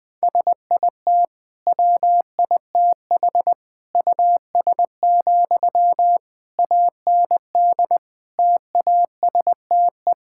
Morse code, heard as a keyed tone, about 20 wpm, 700 Hz. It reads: SIT WITH US, AND TASTE
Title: Sentences from Top 1600 Words 20wpm